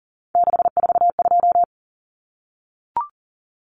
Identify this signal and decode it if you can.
Morse code, heard as a keyed tone, about 38 wpm, 700 Hz. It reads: 642
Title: Numbers - 3-Digit 40wpm